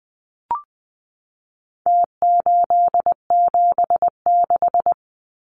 Morse code, heard as a keyed tone, about 20 wpm, 700 Hz. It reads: T876
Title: Numbers - Contest Style - 4-Digit 20wpm